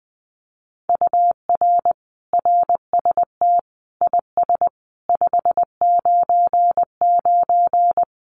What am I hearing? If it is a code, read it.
Morse code, heard as a keyed tone, about 19 wpm, 700 Hz. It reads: UR RST IS 599
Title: Common QSO Phrases Encore 20wpm